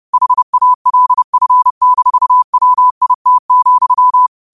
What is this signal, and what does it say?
Morse code, heard as a keyed tone, about 30 wpm, 1000 Hz. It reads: HALF-WIT,
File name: qotdfiles / 30wpm